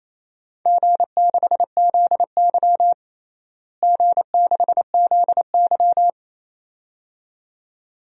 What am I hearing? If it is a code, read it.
Morse code, heard as a keyed tone, about 27 wpm, 700 Hz. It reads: G6ZY G6ZY
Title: DXCC-100 Call Signs - 4 Characters - Repeated 2x - 28wpm